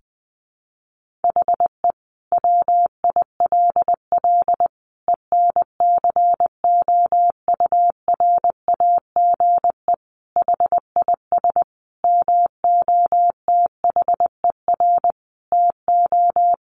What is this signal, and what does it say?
Morse code, heard as a keyed tone, about 20 wpm, 700 Hz. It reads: HE WILL ENCOURAGE HIS MOTHER TO
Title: Sentences from Top 1400 Words 20wpm